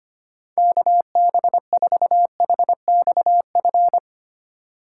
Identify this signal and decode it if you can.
Morse code, heard as a keyed tone, about 31 wpm, 700 Hz. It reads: KB4HXF